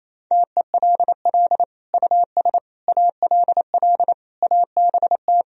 Morse code, heard as a keyed tone, about 28 wpm, 700 Hz. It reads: TELL US ALL ABT
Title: Sentences from Top 200 Words and Abbreviations 28wpm